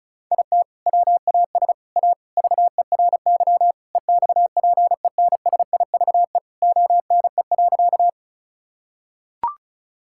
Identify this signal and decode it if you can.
Morse code, heard as a keyed tone, about 36 wpm, 700 Hz. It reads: IT WAS A VERY EXPENSIVE ONE.
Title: Sentences from Top 1000 Words - Review Part 6 - 35wpm